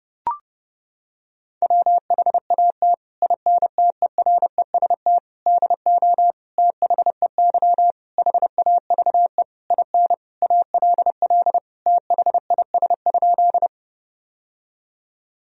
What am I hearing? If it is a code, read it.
Morse code, heard as a keyed tone, about 30 wpm, 700 Hz. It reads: WHAT INTEREST DO THEY HAVE IN ALL THIS?
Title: Sentences from Top 1000 Words - Review Part 2 - 30wpm